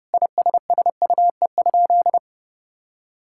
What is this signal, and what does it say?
Morse code, heard as a keyed tone, about 28 wpm, 700 Hz. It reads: ISSUE?
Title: Sentences from Top 1000 Words - Review Part 6 - 30wpm